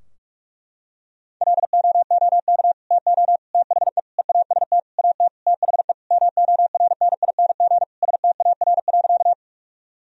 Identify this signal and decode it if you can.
Morse code, heard as a keyed tone, about 46 wpm, 700 Hz. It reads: LOOK TO THE EAST AT THE MORNING STAR.
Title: Sentences from Top 1000 Words - Review Part 3 - 45wpm